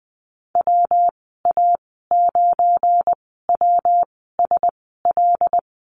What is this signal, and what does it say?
Morse code, heard as a keyed tone, about 19 wpm, spaced as for 14 wpm, 700 Hz. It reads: WA9WSL
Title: 245 Lesson 39g - Letter z - Call Signs 20wpm